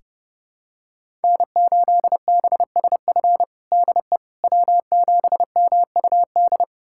Morse code, heard as a keyed tone, about 30 wpm, 700 Hz. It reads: N8BSF DE W7MUD
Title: USA Callsign DE Callsign - 5 Characters - 30wpm